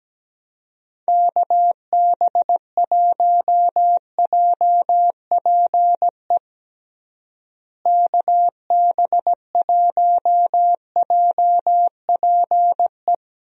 Morse code, heard as a keyed tone, about 17 wpm, 700 Hz. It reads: KB1JPE KB1JPE
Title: USA Call Signs - 6 Characters - Repeated 3x - 17wpm